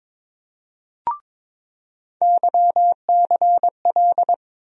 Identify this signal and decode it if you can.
Morse code, heard as a keyed tone, about 22 wpm, 700 Hz. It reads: YCL
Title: USA Call Sign Postfixes - 3 Letters 22wpm